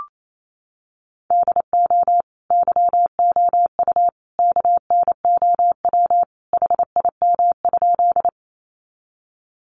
Morse code, heard as a keyed tone, about 28 wpm, 700 Hz. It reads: DO YOU KNOW HIM?
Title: Sentences from Top 100 Words 28wpm